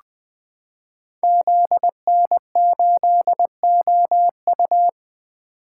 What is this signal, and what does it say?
Morse code, heard as a keyed tone, about 20 wpm, 700 Hz. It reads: ZN8OU
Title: Code Groups - 5 Letter:Numbers - 20wpm